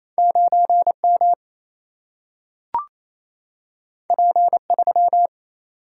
Morse code, heard as a keyed tone, about 28 wpm, 700 Hz. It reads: 9 M P 3
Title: Single-Letter Single-Number 28wpm